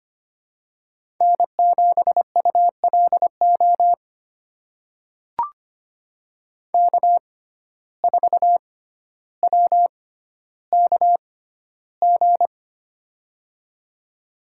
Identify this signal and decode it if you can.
Morse code, heard as a keyed tone, about 25 wpm, spaced as for 13 wpm, 700 Hz. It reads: N7ULO K4WKG
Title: USA Call Signs - 5 Characters - 6x Character Spacing - 25wpm